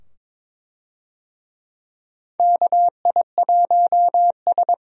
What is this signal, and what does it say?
Morse code, heard as a keyed tone, about 22 wpm, 700 Hz. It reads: KI1S
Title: USA Call Signs - 4 Characters - 22wpm